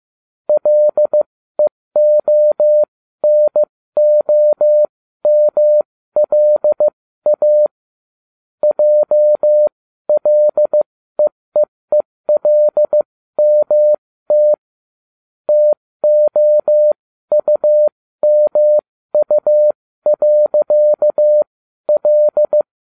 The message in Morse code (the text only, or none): LEONOMLA JLEEELMT TOUMU.L